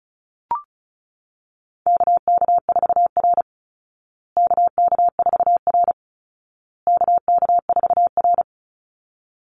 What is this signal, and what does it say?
Morse code, heard as a keyed tone, about 34 wpm, 700 Hz. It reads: KK4R KK4R KK4R
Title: Call Signs Repeated 2x - 35wpm